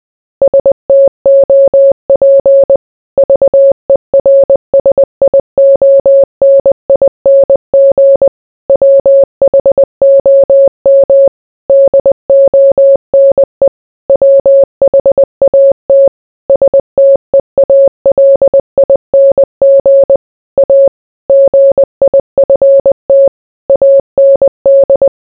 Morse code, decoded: STOP VERSIONING WHOM DONE WHAT STEALING A GIFT AND

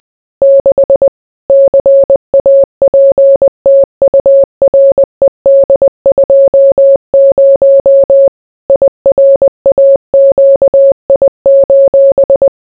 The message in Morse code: 6 CAPTURED 20 IRAQI:

